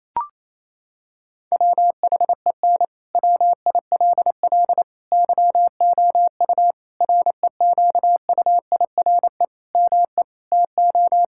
Morse code, heard as a keyed tone, about 27 wpm, 700 Hz. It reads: WHEN WILL YOU REQUIRE ME TO
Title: Sentences from Top 1000 Words - Review Part 6 - 28wpm